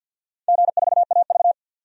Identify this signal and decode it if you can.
Morse code, heard as a keyed tone, about 52 wpm, 700 Hz. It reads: G4AV
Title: DXCC-100 Call Signs - 4 Characters - 50wpm